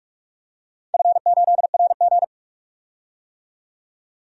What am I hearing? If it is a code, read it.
Morse code, heard as a keyed tone, about 45 wpm, 700 Hz. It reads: F8RG
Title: DXCC-100 Call Signs - 4 Characters - 45wpm